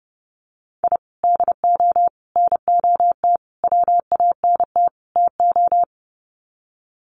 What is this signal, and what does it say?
Morse code, heard as a keyed tone, about 30 wpm, 700 Hz. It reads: I DO NOT WANT TO
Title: Sets of 5 Words 30wpm